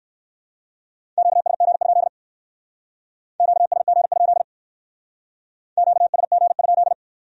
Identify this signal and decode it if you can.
Morse code, heard as a keyed tone, about 56 wpm, 700 Hz. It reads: QSG? QSG? QSG?
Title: Speed-Racing Q-Codes - 45wpm